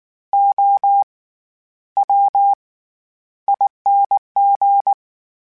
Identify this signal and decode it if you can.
Morse code, heard as a keyed tone, about 19 wpm, spaced as for 11 wpm, 800 Hz. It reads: OWING
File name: Special INGs / Special INGs-mp3s / SlowToFaster ing @18